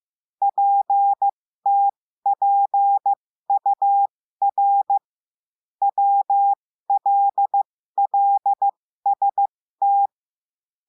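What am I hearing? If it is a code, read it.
Morse code, heard as a keyed tone, about 15 wpm, spaced as for 13 wpm, 800 Hz. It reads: PTPUR WLLST
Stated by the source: Morse code lesson adding the letter W, with a five-letter-group copy drill.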